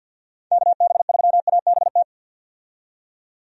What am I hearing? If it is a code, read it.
Morse code, heard as a keyed tone, about 51 wpm, 700 Hz. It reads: KB3ABT
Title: USA Call Signs - 6 Characters - 50wpm